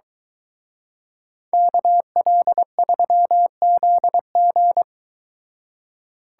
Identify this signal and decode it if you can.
Morse code, heard as a keyed tone, about 23 wpm, 700 Hz. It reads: KL3ZG